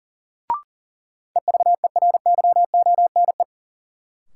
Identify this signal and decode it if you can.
Morse code, heard as a keyed tone, about 40 wpm, 700 Hz. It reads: EVERYONE
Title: Sentences from Top 800 Words - New Word Drill - 40wpm